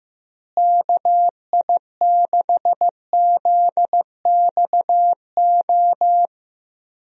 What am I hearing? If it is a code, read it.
Morse code, heard as a keyed tone, about 15 wpm, 700 Hz. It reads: KI6ZXO
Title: USA Call Signs - 6 Characters - Repeated 3x - 15wpm